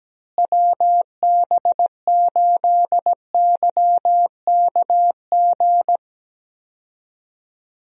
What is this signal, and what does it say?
Morse code, heard as a keyed tone, about 17 wpm, 700 Hz. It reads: WB8YKG
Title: USA Call Signs - 6 Characters - Repeated 2x - 17wpm